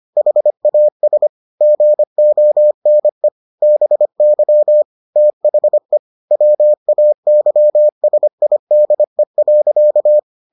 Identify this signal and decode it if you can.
Morse code, heard as a keyed tone, about 25 wpm, 600 Hz. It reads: HAS GONE BY THE WAYSIDE.